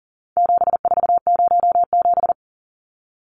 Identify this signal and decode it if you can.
Morse code, heard as a keyed tone, about 40 wpm, 700 Hz. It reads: 7407
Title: Numbers - 4-Digit 40wpm